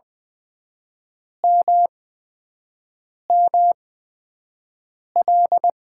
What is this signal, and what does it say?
Morse code, heard as a keyed tone, about 20 wpm, spaced as for 5 wpm, 700 Hz. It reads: MML
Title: USA Call Signs - 5 Characters - 8x Character Spacing - 20wpm